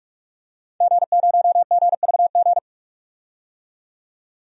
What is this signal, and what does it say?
Morse code, heard as a keyed tone, about 46 wpm, 700 Hz. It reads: G0GVG
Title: DXCC-100 Call Signs - 5 Characters - 45wpm